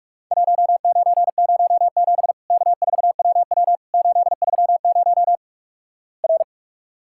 Morse code, heard as a keyed tone, about 46 wpm, 700 Hz. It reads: 1907 K4WW 830 R
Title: Worked All Europe DX Contest - QTC Exchange - 45wpm